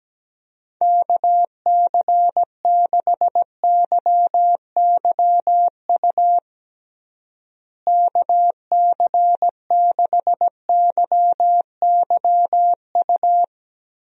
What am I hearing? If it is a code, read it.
Morse code, heard as a keyed tone, about 17 wpm, 700 Hz. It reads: KC6YYU KC6YYU
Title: USA Call Signs - 6 Characters - Repeated 2x - 17wpm